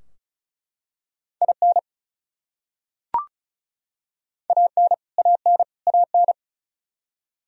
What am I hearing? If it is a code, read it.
Morse code, heard as a keyed tone, about 34 wpm, 700 Hz. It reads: IN AN AN AN
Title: Top 100 Words Repeated 35wpm